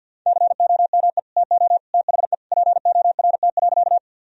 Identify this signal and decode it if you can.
Morse code, heard as a keyed tone, about 52 wpm, 700 Hz. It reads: COME TO THE PORT.
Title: Sentences from Top 1000 Words - Review Part 1 - 50wpm